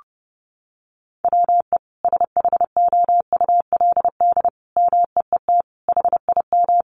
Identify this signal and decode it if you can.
Morse code, heard as a keyed tone, about 30 wpm, 700 Hz. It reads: WE SHOULD MEET HIM
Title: Sentences from Top 1100 Words 30wpm